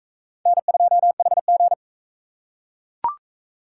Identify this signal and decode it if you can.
Morse code, heard as a keyed tone, about 43 wpm, 700 Hz. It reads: N2HG